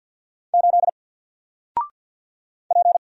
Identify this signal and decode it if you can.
Morse code, heard as a keyed tone, about 48 wpm, 700 Hz. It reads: 8 P
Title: Single Letter-Number Alt 50wpm